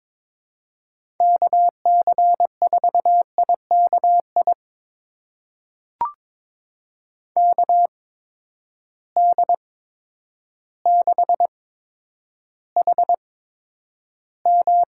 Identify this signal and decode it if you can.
Morse code, heard as a keyed tone, about 22 wpm, spaced as for 10 wpm, 700 Hz. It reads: KC4IKI KD6HM
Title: USA Call Signs - 6 Characters - 8x Character Spacing - 22wpm